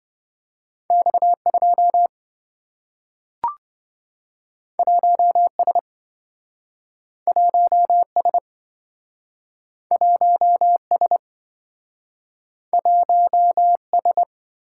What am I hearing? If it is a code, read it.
Morse code, heard as a keyed tone, about 27 wpm, 700 Hz. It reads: X2 1S 1S 1S 1S